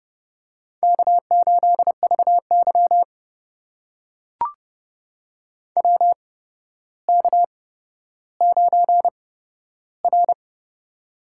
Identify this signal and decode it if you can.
Morse code, heard as a keyed tone, about 30 wpm, spaced as for 13 wpm, 700 Hz. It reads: K8VY WK9R
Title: USA Call Signs - 4 Characters - 8x Character Spacing - 30wpm